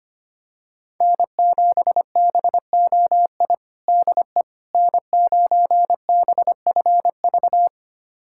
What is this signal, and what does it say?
Morse code, heard as a keyed tone, about 25 wpm, 700 Hz. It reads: N7BOI DE N9BFV
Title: USA Callsign DE Callsign - 5 Characters - 25wpm